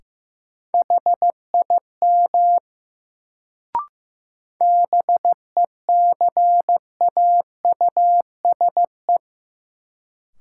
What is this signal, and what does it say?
Morse code, heard as a keyed tone, about 15 wpm, 700 Hz. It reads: HIM BECAUSE
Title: Sets of 2 Words - New Word Drill - 15wpm